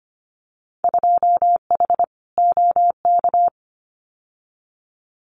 Morse code, heard as a keyed tone, about 25 wpm, 700 Hz. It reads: 2H OK
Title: Winter Field Day Exchanges Encore 25wpm